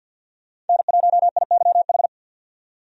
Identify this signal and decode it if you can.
Morse code, heard as a keyed tone, about 52 wpm, 700 Hz. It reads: N1IYH
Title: USA Call Signs - 5 Characters - 50wpm